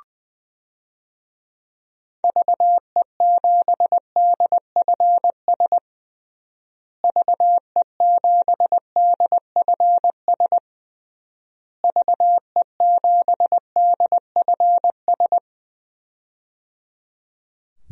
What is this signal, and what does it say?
Morse code, heard as a keyed tone, about 20 wpm, 700 Hz. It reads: VE7DFS VE7DFS VE7DFS
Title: DXCC-100 Call Signs - 6 Characters - Repeated 3x - 20wpm